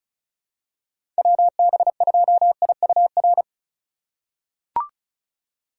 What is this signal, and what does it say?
Morse code, heard as a keyed tone, about 36 wpm, 700 Hz. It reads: WB2IUR
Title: USA Call Signs - 4x Character Spacing - 35wpm